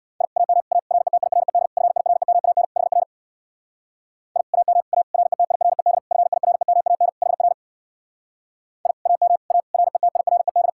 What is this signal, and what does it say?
Morse code, heard as a keyed tone, about 64 wpm, 700 Hz. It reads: I AM A RETIRED FIREMAN HR I AM A RETIRED FIREMAN HR I AM A RETIRED